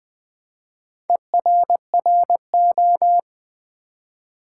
Morse code, heard as a keyed tone, about 20 wpm, 700 Hz. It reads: ERRO
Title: Build A Word - 1 Letter At A Time - 5-Letter Words - 20wpm